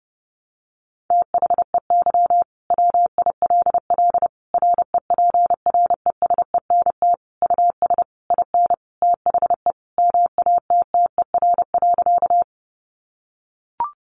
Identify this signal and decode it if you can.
Morse code, heard as a keyed tone, about 30 wpm, 700 Hz. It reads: THEY WILL REPRESENT US IN THE MATTER.
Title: Sentences from Top 1200 Words 30wpm